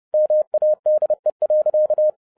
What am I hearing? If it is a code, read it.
Morse code, heard as a keyed tone, about 30 wpm, 610 Hz. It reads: MADE.